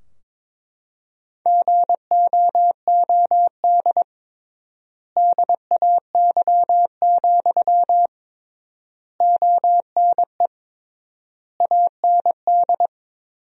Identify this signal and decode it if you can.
Morse code, heard as a keyed tone, about 22 wpm, 700 Hz. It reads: GOOD DAY, ONE AND
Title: 3x Word-Spacing - Sentences from Top 200 Words - 22wpm